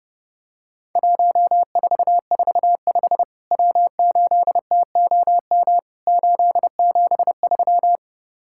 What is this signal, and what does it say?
Morse code, heard as a keyed tone, about 30 wpm, 700 Hz. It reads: 1445 W8TOM 873
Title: Worked All Europe DX Contest - QTC Exchange - 30wpm